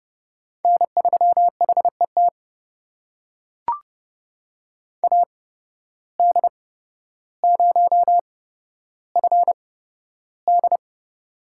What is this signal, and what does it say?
Morse code, heard as a keyed tone, about 30 wpm, spaced as for 13 wpm, 700 Hz. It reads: N3HET AD0FD
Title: USA Call Signs - 8x Character Spacing - 30wpm